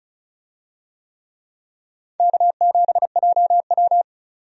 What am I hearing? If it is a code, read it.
Morse code, heard as a keyed tone, about 36 wpm, 700 Hz. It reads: K7JW
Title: USA Call Signs - 4 Characters - 35wpm